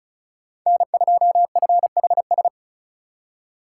Morse code, heard as a keyed tone, about 34 wpm, 700 Hz. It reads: N2FHS
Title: USA Call Signs - 2x Character Spacing - 35wpm